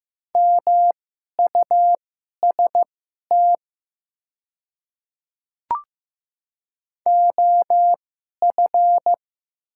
Morse code, heard as a keyed tone, about 15 wpm, spaced as for 11 wpm, 690 Hz. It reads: MUST OF
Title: 2x Character Spacing - Top 500 Words - 15wpm